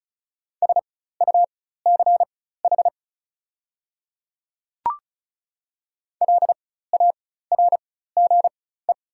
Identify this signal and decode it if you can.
Morse code, heard as a keyed tone, about 36 wpm, spaced as for 16 wpm, 700 Hz. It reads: SUCH LARGE
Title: 4x Character Spacing - Top 200 Words - 35wpm